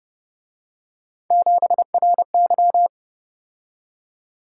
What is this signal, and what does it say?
Morse code, heard as a keyed tone, about 32 wpm, 700 Hz. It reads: 7RY